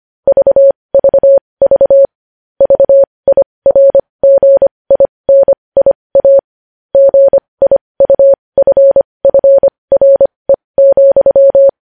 Morse code, heard as a keyed tone, about 25 wpm, spaced as for 20 wpm, 550 Hz. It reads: VVV VIRGINIA GIUFFRE,
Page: news headlines in morse code at 20 wpm